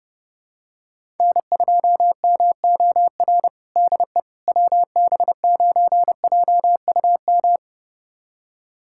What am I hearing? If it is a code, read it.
Morse code, heard as a keyed tone, about 30 wpm, 700 Hz. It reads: N2MOR DE WB9JUM
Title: USA Callsign DE Callsign 30wpm